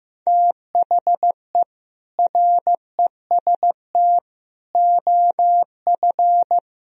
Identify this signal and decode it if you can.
Morse code, heard as a keyed tone, about 15 wpm, 690 Hz. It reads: THE REST OF
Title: Speed-Racing - Sets of 5 Words 15wpm